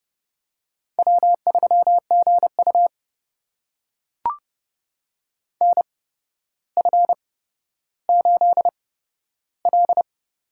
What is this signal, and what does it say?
Morse code, heard as a keyed tone, about 30 wpm, spaced as for 13 wpm, 700 Hz. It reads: W3GU NF8L
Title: USA Call Signs - 4 Characters - 8x Character Spacing - 30wpm